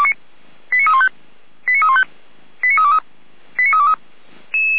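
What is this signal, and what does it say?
Five-tone selective call (ZVEI), heard as quick runs of five tones, tones of about 70 ms. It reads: 89315 89315 89323 89323
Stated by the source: ZVEI five-tone alert received by SDR, from Feuerwehr Orlamünde's audio archive.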